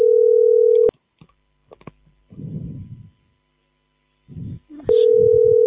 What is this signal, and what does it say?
Ringback tone (North America), 440 + 480 Hz.